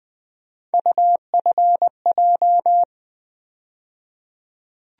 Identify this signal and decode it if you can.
Morse code, heard as a keyed tone, about 20 wpm, 700 Hz. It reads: UFJ
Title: USA Call Sign Postfixes - 3 Letters 20wpm